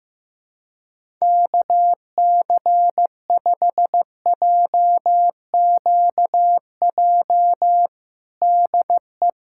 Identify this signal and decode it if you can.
Morse code, heard as a keyed tone, about 15 wpm, 700 Hz. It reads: KC5JQJ DE
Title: USA Callsign DE Callsign - 6 Characters - 15wpm